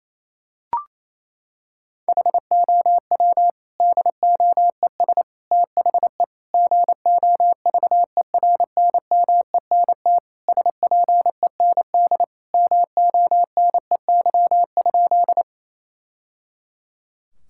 Morse code, heard as a keyed tone, about 28 wpm, 700 Hz. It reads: HOW DOES THE GOVERNMENT SPEND MONEY?
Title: Sentences from Top 1000 Words - Review Part 5 - 28wpm